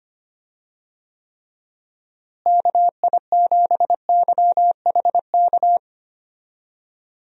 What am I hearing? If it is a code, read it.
Morse code, heard as a keyed tone, about 25 wpm, 700 Hz. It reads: KI7YHK
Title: USA Call Signs - 6 Characters - Repeated 3x - 25wpm